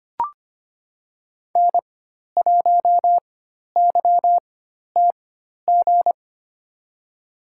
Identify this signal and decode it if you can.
Morse code, heard as a keyed tone, about 25 wpm, spaced as for 12 wpm, 700 Hz. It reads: N1YTG
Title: USA Call Signs - 5 Characters - 4x Character Spacing - 25wpm